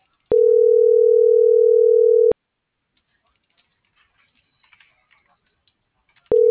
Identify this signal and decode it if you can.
Ringback tone (North America), 440 + 480 Hz.
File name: CENTRAL MULTIMODAL / out josimar.cruz - .76301